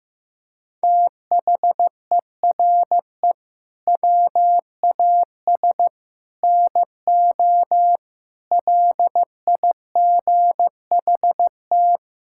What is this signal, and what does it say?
Morse code, heard as a keyed tone, about 15 wpm, 700 Hz. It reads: THERE WAS NO LIGHT
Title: Sentences from Top 1000 Words - Review Part 5 - 15wpm